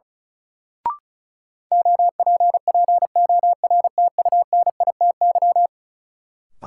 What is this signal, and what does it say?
Morse code, heard as a keyed tone, about 36 wpm, 700 Hz. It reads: OPPORTUNITY
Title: Sentences from Top 1000 Words - New Word Drill - 35wpm